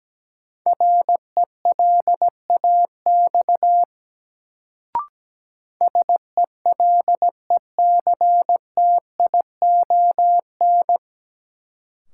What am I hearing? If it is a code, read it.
Morse code, heard as a keyed tone, about 17 wpm, 700 Hz. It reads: RELAX SELECTION
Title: Sentences from Top 1400 Words - New Word Drill - 17wpm